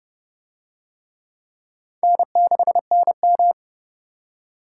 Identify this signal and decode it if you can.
Morse code, heard as a keyed tone, about 30 wpm, 700 Hz. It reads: N6NM